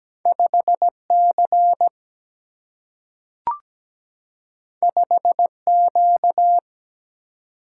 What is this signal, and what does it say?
Morse code, heard as a keyed tone, about 17 wpm, 700 Hz. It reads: 5C 5Q
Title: Single-Letter Single-Number 17wpm